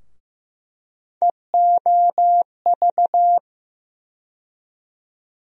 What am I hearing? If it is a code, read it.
Morse code, heard as a keyed tone, about 15 wpm, 700 Hz. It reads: EOV
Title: USA Call Sign Postfixes - 3 Letters 15wpm